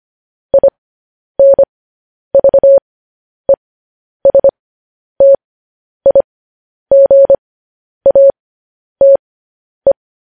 Morse code, heard as a keyed tone, about 25 wpm, spaced as for 10 wpm, 550 Hz. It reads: INVESTIGATE
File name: podcast / News headlines-1x-25x10